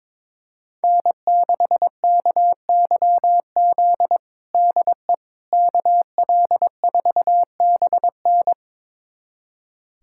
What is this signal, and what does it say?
Morse code, heard as a keyed tone, about 22 wpm, 700 Hz. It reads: N6KYZ DE KL4BN
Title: USA Callsign DE Callsign - 5 Characters - 22wpm